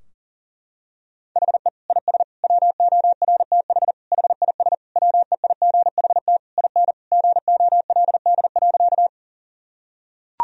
Morse code, heard as a keyed tone, about 40 wpm, 700 Hz. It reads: HE IS WORTH HIS WEIGHT IN GOLD.
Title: Sentences from Top 1000 Words - Review Part 3 - 40wpm